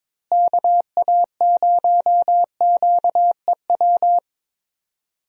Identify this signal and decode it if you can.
Morse code, heard as a keyed tone, about 22 wpm, 700 Hz. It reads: KA0QEW
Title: USA Call Signs - 6 Characters - Repeated 2x - 22wpm